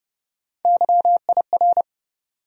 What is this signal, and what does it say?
Morse code, heard as a keyed tone, about 32 wpm, 700 Hz. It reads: YIR
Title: USA Call Sign Postfixes - 3 Letters 30wpm